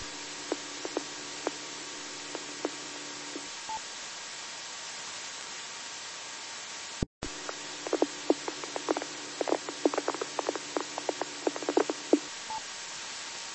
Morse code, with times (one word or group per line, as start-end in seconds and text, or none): none